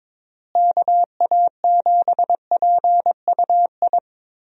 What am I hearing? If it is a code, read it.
Morse code, heard as a keyed tone, about 22 wpm, 700 Hz. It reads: KA7PUI
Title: USA Call Signs - 6 Characters - Repeated 3x - 22wpm